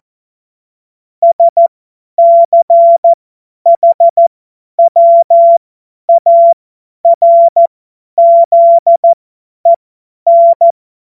Morse code, heard as a keyed tone, about 14 wpm, spaced as for 10 wpm, 680 Hz. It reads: SCHWARZEN
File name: Morsekurs / Witze / Witz 16 Januar 2024 50 F70